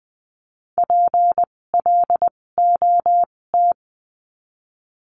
Morse code, heard as a keyed tone, about 20 wpm, spaced as for 16 wpm, 700 Hz. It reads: PLOT
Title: 256 Lesson 42a - Words 12-20wpm